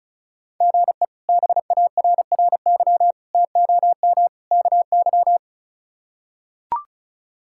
Morse code, heard as a keyed tone, about 34 wpm, 700 Hz. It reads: GE BARRY TOM KY
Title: K1USN SST QSO Practice - 35wpm